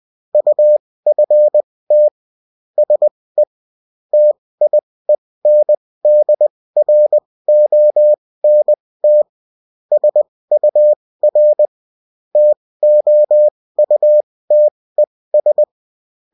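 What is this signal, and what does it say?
Morse code, heard as a keyed tone, about 20 wpm, spaced as for 16 wpm, 600 Hz. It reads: UFT SE TIENDRONT SUR TOUTES
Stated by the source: Bulletin in text and CW form.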